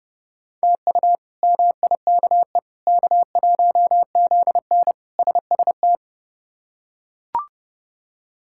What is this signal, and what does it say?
Morse code, heard as a keyed tone, about 30 wpm, 700 Hz. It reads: TU MIKE K1ZN SST
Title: K1USN SST QSO Practice - 30wpm